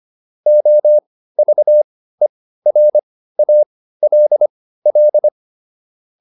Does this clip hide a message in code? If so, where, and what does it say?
Yes: Morse code from 0.5 to 5.3 s, heard as a keyed tone, about 25 wpm, spaced as for 15 wpm, 600 Hz. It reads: OVERALL